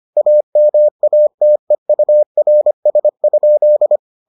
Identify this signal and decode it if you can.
Morse code, heard as a keyed tone, about 25 wpm, 600 Hz. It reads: AMATEURS?